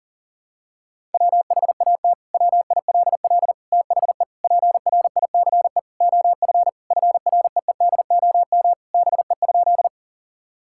Morse code, heard as a keyed tone, about 40 wpm, 700 Hz. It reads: WHAT WILL THE PRICE OF FREEDOM BE?
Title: Sentences from Top 1000 Words - Review Part 5 - 40wpm